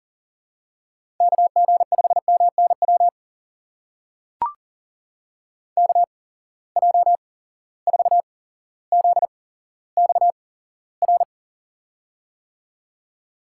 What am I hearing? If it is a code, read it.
Morse code, heard as a keyed tone, about 40 wpm, spaced as for 17 wpm, 700 Hz. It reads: KG5MNW KJ4ZXR
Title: USA Call Signs - 6 Characters - 8x Character Spacing - 40wpm